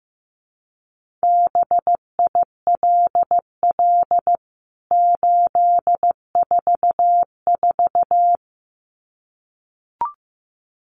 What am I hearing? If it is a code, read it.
Morse code, heard as a keyed tone, about 15 wpm, 700 Hz. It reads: BILL 844
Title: CWT Exchange 15wpm